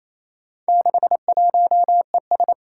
Morse code, heard as a keyed tone, about 27 wpm, 700 Hz. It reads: 61ES
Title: Four Challenging Letter-Number Combinations 28wpm